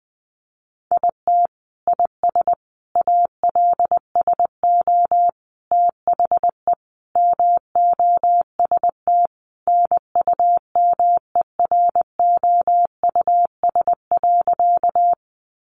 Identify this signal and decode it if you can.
Morse code, heard as a keyed tone, about 19 wpm, 700 Hz. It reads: IT IS ALSO THE MOST NUMEROUS.
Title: Sentences from Top 1400 Words 20wpm